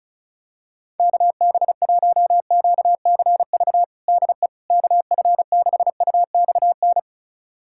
Morse code, heard as a keyed tone, about 36 wpm, 700 Hz. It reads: KB1QCV DE KF6UXN
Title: USA Callsign DE Callsign - 6 Characters - 35wpm